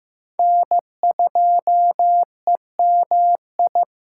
Morse code, heard as a keyed tone, about 18 wpm, 700 Hz. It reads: N2EMI
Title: Speed Racing - USA Call Signs 15wpm